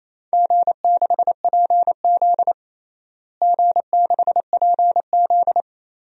Morse code, heard as a keyed tone, about 28 wpm, 700 Hz. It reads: G6PZ G6PZ
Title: DXCC-100 Call Signs - 4 Characters - Repeated 3x - 28wpm